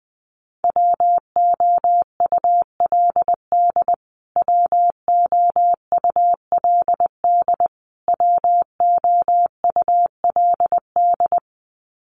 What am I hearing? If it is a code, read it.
Morse code, heard as a keyed tone, about 20 wpm, 700 Hz. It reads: WOULD WOULD WOULD
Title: Top 200 Words Repeated 20wpm